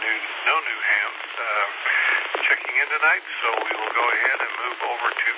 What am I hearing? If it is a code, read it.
Emergency Alert System (SAME) tones, heard as harsh buzzing bursts, with speech. It is heard from 0.7 to 4.5 s.